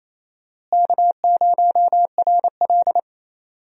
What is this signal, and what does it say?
Morse code, heard as a keyed tone, about 28 wpm, 700 Hz. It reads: K0RL
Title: USA Call Signs - 4 Characters - Repeated 2x - 28wpm